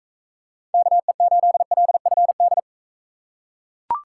Morse code, heard as a keyed tone, about 39 wpm, 700 Hz. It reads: KE8LFD